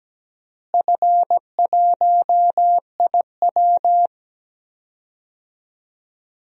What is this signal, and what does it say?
Morse code, heard as a keyed tone, about 17 wpm, 700 Hz. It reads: F1IW
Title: DXCC-100 Call Signs - 4 Characters - Repeated 3x - 17wpm